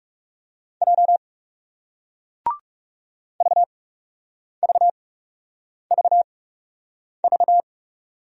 Morse code, heard as a keyed tone, about 40 wpm, spaced as for 8 wpm, 700 Hz. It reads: J VVVV